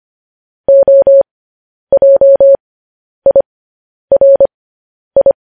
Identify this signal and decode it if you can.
Morse code, heard as a keyed tone, about 25 wpm, spaced as for 10 wpm, 550 Hz. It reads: OJIRI
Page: news-headlines-in-morse-code-at-10-wpm